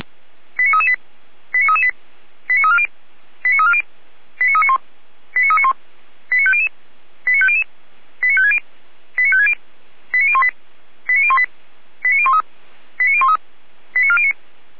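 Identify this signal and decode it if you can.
Five-tone selective call (ZVEI), heard as quick runs of five tones, tones of about 80 ms. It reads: 89338 89338 89340 89340 89391 89391 89500 89500 89560 89560 89018 89018 89013 89013 89409
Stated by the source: ZVEI five-tone selective call 89304 received on SDR.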